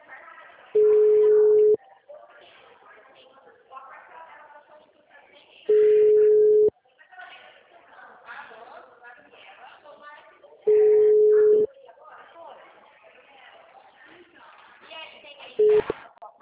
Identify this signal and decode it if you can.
Ringback tone (Europe, 425 Hz), with speech.
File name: out- -2007- - - .45341